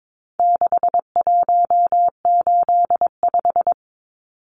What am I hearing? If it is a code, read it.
Morse code, heard as a keyed tone, about 22 wpm, 700 Hz. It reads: 6185
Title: Numbers - 4-Digit 22wpm